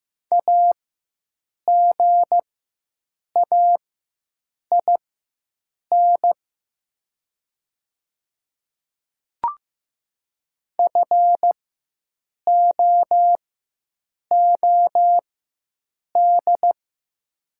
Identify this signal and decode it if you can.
Morse code, heard as a keyed tone, about 15 wpm, spaced as for 7 wpm, 700 Hz. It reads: AGAIN FOOD
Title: 4x Character Spacing - Top 300 Words - 15wpm